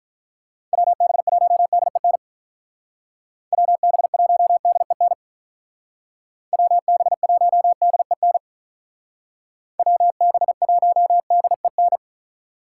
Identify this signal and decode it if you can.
Morse code, heard as a keyed tone, about 43 wpm, 700 Hz. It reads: WB1DEN WB1DEN WB1DEN WB1DEN
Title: Speed Racing - USA Call Signs 35wpm